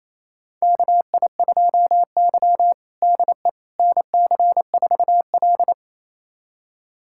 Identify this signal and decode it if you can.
Morse code, heard as a keyed tone, about 27 wpm, 700 Hz. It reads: KI2Y DE NC4L
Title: USA Callsign DE Callsign - 4 Characters - 28wpm